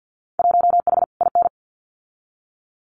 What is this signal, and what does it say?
Morse code, heard as a keyed tone, about 50 wpm, 700 Hz. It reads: 1H IN
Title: Winter Field Day Exchanges 50wpm